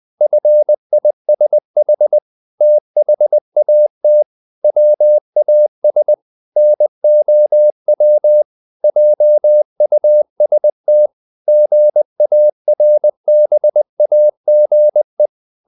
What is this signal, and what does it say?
Morse code, heard as a keyed tone, about 20 wpm, 600 Hz. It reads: FISH THAT WAS NOW JUST GARBAGE
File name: the old man and the sea part 2 20wpm